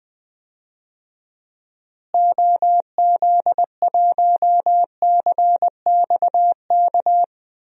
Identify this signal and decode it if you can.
Morse code, heard as a keyed tone, about 19 wpm, 700 Hz. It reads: OZ1CXK
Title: DXCC-100 Call Signs - 6 Characters - Repeated 3x - 20wpm